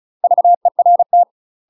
Morse code, heard as a keyed tone, about 34 wpm, 700 Hz. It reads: VERT
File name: UK MASTER QSOs TRAINING / 35WPM / UK MASTER QSOs 0034